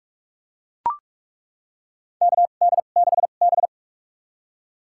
Morse code, heard as a keyed tone, about 42 wpm, spaced as for 31 wpm, 700 Hz. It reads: KD6B
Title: USA Call Signs - 4 Characters - 2x Character Spacing - 45wpm